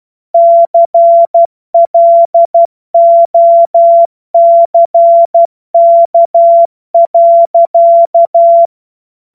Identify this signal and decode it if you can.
Morse code, heard as a keyed tone, about 12 wpm, 680 Hz. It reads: CLOCK.
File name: Morsekurs / Witze